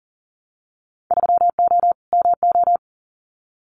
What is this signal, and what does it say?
Morse code, heard as a keyed tone, about 40 wpm, 700 Hz. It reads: 3O MO
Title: Winter Field Day Exchanges Encore 40wpm